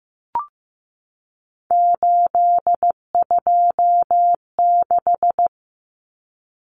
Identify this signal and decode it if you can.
Morse code, heard as a keyed tone, about 15 wpm, 700 Hz. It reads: 826
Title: Common Cut Numbers - 3-Digit 15wpm